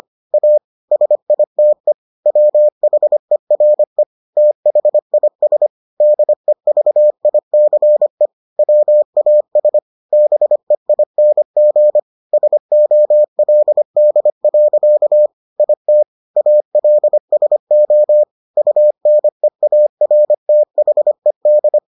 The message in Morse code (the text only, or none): A SITE WHERE THIS DEVICE WAS BEING SOLD. IT ALSO UNEARTHED